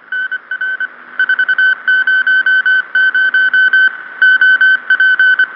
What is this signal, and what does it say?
Morse code, heard as a keyed tone, about 25 wpm, 1500 Hz. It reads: NR 400 OP